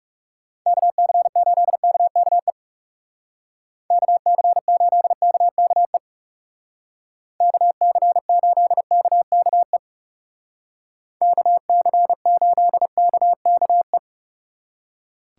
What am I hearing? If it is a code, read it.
Morse code, heard as a keyed tone, about 37 wpm, 700 Hz. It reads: KC8KKE KC8KKE KC8KKE KC8KKE
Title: Speed Racing - USA 2x3 Call Signs 30wpm